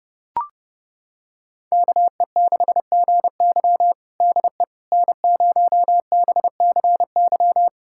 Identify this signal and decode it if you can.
Morse code, heard as a keyed tone, about 30 wpm, 700 Hz. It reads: KE6GY DE N0BCY
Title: USA Callsign DE Callsign 30wpm